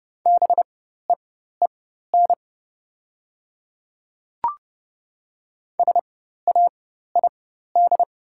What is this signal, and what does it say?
Morse code, heard as a keyed tone, about 30 wpm, spaced as for 14 wpm, 700 Hz. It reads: BEEN SAID
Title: 4x Character Spacing - Top 200 Words - 30wpm